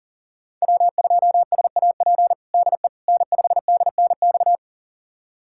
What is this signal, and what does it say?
Morse code, heard as a keyed tone, about 40 wpm, 700 Hz. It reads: W2SAP DE N5DNX
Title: USA Callsign DE Callsign - 5 Characters - 40wpm